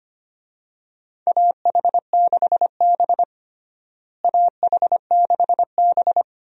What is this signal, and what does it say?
Morse code, heard as a keyed tone, about 25 wpm, 700 Hz. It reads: AH6B AH6B
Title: USA Call Signs - 4 Characters - Repeated 2x - 25wpm